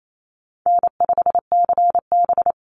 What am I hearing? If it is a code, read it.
Morse code, heard as a keyed tone, about 28 wpm, 700 Hz. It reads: N5CB
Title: USA Call Signs 28wpm